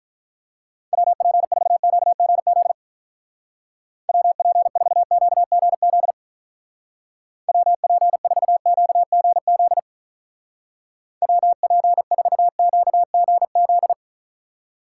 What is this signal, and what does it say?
Morse code, heard as a keyed tone, about 45 wpm, 700 Hz. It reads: WP4QGZ WP4QGZ WP4QGZ WP4QGZ